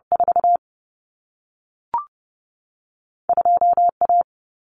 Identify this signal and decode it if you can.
Morse code, heard as a keyed tone, about 30 wpm, 700 Hz. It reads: 4 2 A
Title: Common Cut Numbers - 2-Digit 30wpm